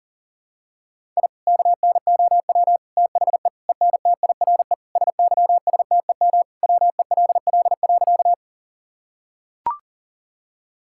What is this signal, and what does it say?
Morse code, heard as a keyed tone, about 40 wpm, 700 Hz. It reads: I KNOW THE ENTIRE SYSTEM WELL.
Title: Sentences from Top 1000 Words - Review Part 5 - 40wpm